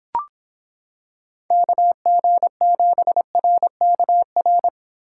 Morse code, heard as a keyed tone, about 26 wpm, 700 Hz. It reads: KG7RKR